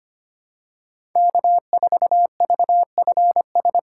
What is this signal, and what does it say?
Morse code, heard as a keyed tone, about 25 wpm, 700 Hz. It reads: K4VFS
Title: USA Call Signs - 5 Characters - 2x Character Spacing - 25wpm